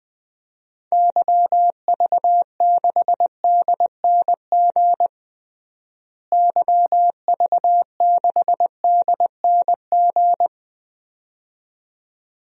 Morse code, heard as a keyed tone, about 20 wpm, 700 Hz. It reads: YV6DNG YV6DNG
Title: DXCC-100 Call Signs - 6 Characters - Repeated 2x - 20wpm